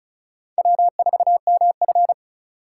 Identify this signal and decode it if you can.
Morse code, heard as a keyed tone, about 34 wpm, 700 Hz. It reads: W4MF
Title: USA Call Signs - 4 Characters - 2x Character Spacing - 35wpm